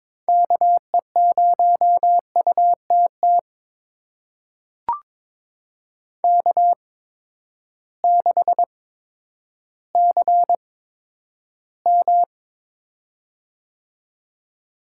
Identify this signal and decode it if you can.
Morse code, heard as a keyed tone, about 22 wpm, spaced as for 11 wpm, 700 Hz. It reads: KE0UTT K6CM
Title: USA Call Signs - 8x Character Spacing - 22wpm